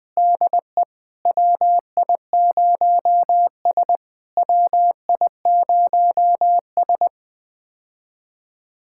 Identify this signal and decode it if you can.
Morse code, heard as a keyed tone, about 20 wpm, 700 Hz. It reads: DE WI0S WI0S
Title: POTA QSO Practice - 20wpm